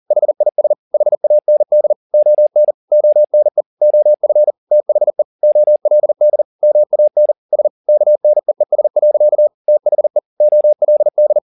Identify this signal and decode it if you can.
Morse code, heard as a keyed tone, about 40 wpm, 600 Hz. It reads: HIS HAND ON ONE OF THE OLD MAN S KNEES. THE OLD